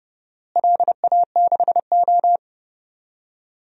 Morse code, heard as a keyed tone, about 32 wpm, 700 Hz. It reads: LA6O
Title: DXCC-100 Call Signs - 4 Characters - Repeated 2x - 30wpm